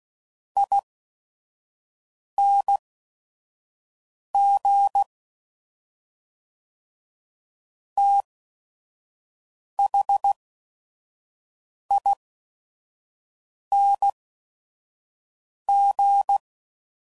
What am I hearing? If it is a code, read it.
Morse code, heard as a keyed tone, about 16 wpm, spaced as for 5 wpm, 800 Hz. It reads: ING THING
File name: Suffixes / Special ING's / Sngl ING's